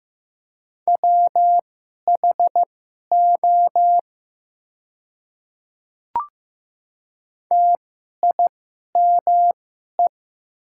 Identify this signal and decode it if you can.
Morse code, heard as a keyed tone, about 15 wpm, spaced as for 11 wpm, 700 Hz. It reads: WHO TIME
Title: 2x Character Spacing - Top 200 Words - 15wpm